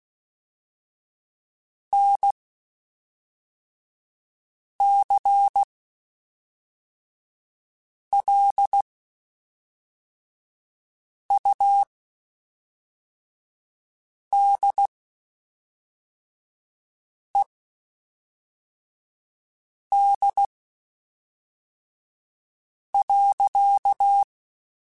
Morse code, heard as a keyed tone, about 16 wpm, spaced as for 3 wpm, 800 Hz. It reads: NCLUDED.
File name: Quotes Spaced / Quotes Spaced-mp3s / Airtight sp @5